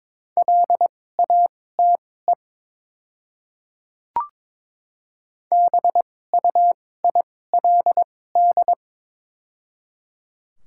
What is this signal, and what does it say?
Morse code, heard as a keyed tone, about 22 wpm, spaced as for 16 wpm, 700 Hz. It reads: LATE BUILD
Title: 2x Character Spacing - Top 500 Words - 22wpm